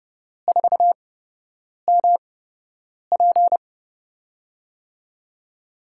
Morse code, heard as a keyed tone, about 30 wpm, spaced as for 8 wpm, 700 Hz. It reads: 4MP
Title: USA Call Signs - 4 Characters - 8x Character Spacing - 30wpm